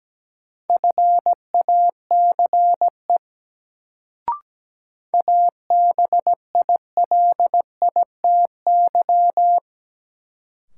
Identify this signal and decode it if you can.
Morse code, heard as a keyed tone, about 17 wpm, 700 Hz. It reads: FACE ABILITY
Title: Sets of 3 Words - New Word Drill - 17wpm